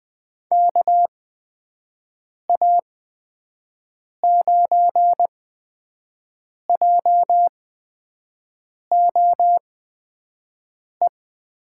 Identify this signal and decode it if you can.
Morse code, heard as a keyed tone, about 20 wpm, spaced as for 5 wpm, 700 Hz. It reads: KA9JOE
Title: USA Call Signs - 6 Characters - 8x Character Spacing - 20wpm